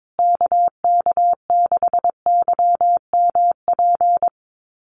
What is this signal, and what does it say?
Morse code, heard as a keyed tone, about 22 wpm, 700 Hz. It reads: KK6YMP
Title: USA Call Signs Repeated 3x 22wpm